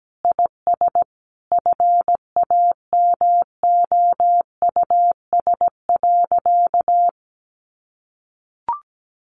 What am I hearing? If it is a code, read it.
Morse code, heard as a keyed tone, about 17 wpm, 700 Hz. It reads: IS FAMOUS.
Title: Sentences from Top 1500 Words 17wpm